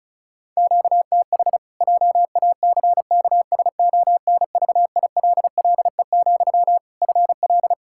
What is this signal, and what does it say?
Morse code, heard as a keyed tone, about 36 wpm, 700 Hz. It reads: QTH JACKSONVILLE, FL
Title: Practice QSO - Exchange 1 - Beginning - 35wpm